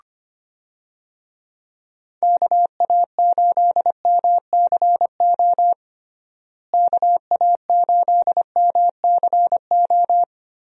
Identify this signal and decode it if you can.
Morse code, heard as a keyed tone, about 25 wpm, 700 Hz. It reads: KA8MCO KA8MCO
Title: USA Call Signs - 6 Characters - Repeated 3x - 25wpm